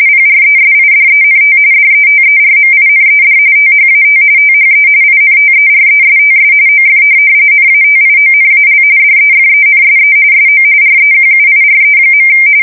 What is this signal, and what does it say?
RTTY, about 45.45 baud. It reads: PRESIDENT DETERMINED THAT THE INTERESTS OF NATIONAL DEFENSE SO REQUIRE.
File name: fsk170 / USS-New-Jersey